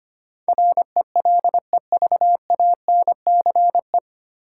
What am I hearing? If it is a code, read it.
Morse code, heard as a keyed tone, about 25 wpm, 700 Hz. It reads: RELEVANCE
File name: Nine Letter Words - 25wpm